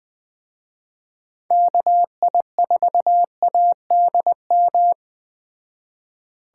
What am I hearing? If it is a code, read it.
Morse code, heard as a keyed tone, about 20 wpm, 700 Hz. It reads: KI4ADM